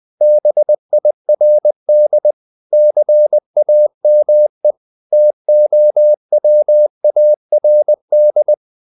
Morse code, heard as a keyed tone, about 20 wpm, 600 Hz. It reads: BIRD CAME TOWARD
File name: the old man and the sea / the old man and the sea part 1 20wpm